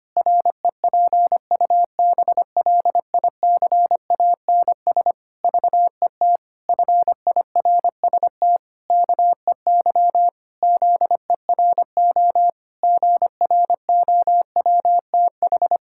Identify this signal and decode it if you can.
Morse code, heard as a keyed tone, about 25 wpm, 700 Hz. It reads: REPUBLICANS VET FIRST KEY ZERO GROWTH
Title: Pangram Challenge v2 - 25wpm